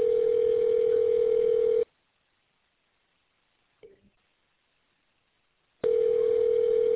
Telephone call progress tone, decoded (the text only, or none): ringback tone